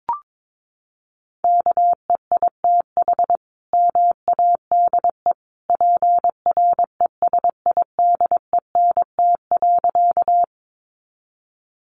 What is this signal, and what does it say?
Morse code, heard as a keyed tone, about 22 wpm, 700 Hz. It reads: KEITH MADE PRESIDENT.
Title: A Sound Of Thunder 22wpm